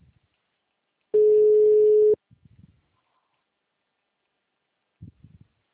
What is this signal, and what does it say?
Ringback tone (Europe, 425 Hz).
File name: out- -2011- - - .11136